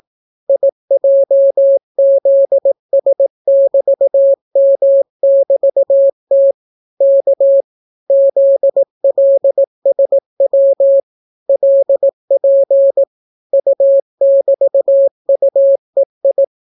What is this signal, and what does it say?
Morse code, heard as a keyed tone, about 18 wpm, 550 Hz. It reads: IJZS<BT>M<BT>T K ZLSW LP U<BT>UEI